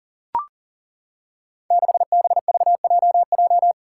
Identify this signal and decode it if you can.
Morse code, heard as a keyed tone, about 38 wpm, 700 Hz. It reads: 6BVJJ
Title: Five Challenging Letter-Number Combinations 40wpm